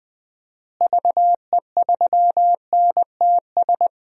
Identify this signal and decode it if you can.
Morse code, heard as a keyed tone, about 20 wpm, 700 Hz. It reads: VE3NTS
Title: DXCC-100 Call Signs - 6 Characters - Repeated 3x - 20wpm